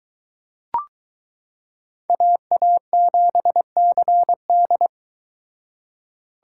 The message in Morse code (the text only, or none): AA7CD